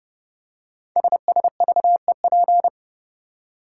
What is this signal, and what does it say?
Morse code, heard as a keyed tone, about 30 wpm, 690 Hz. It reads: SSVEP
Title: Code Groups - 5 Letters - 30wpm